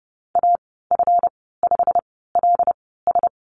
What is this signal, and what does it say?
Morse code, heard as a keyed tone, about 30 wpm, spaced as for 17 wpm, 700 Hz. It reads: AF5LS
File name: Lesson 37g - Letter x - Call Signs 30wpm